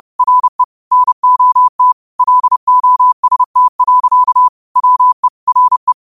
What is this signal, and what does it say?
Morse code, heard as a keyed tone, about 30 wpm, 1000 Hz. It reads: RE NOT LOST. WE RE
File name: qotdfiles / 30wpm